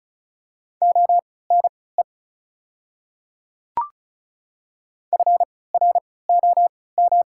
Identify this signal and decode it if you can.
Morse code, heard as a keyed tone, about 36 wpm, spaced as for 20 wpm, 690 Hz. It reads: ONE FROM
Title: 3x Character Spacing - Top 100 Words - 35wpm